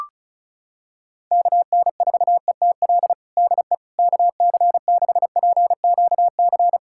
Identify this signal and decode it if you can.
Morse code, heard as a keyed tone, about 36 wpm, 700 Hz. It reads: KN4ETL DE KC6PQC
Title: USA Callsign DE Callsign 35wpm